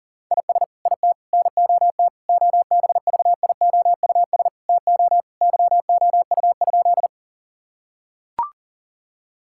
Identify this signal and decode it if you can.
Morse code, heard as a keyed tone, about 40 wpm, 700 Hz. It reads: IS IT NOT OBVIOUS TO YOU?
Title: Sentences from Top 1600 Words 40wpm